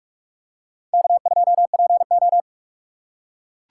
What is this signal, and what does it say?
Morse code, heard as a keyed tone, about 42 wpm, 700 Hz. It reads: K2PO